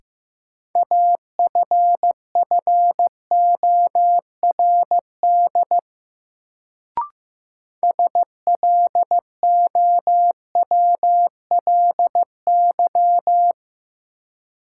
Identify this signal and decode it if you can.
Morse code, heard as a keyed tone, about 15 wpm, 700 Hz. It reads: AFFORD SLOWLY
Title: Sentences from Top 1100 Words - New Word Drill Alt - 15wpm